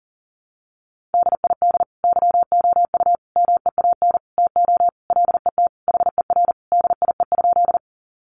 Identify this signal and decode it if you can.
Morse code, heard as a keyed tone, about 37 wpm, 700 Hz. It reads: DID YOU MEAN TO LET HER DIE?
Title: Sentences from Top 1300 Words 40wpm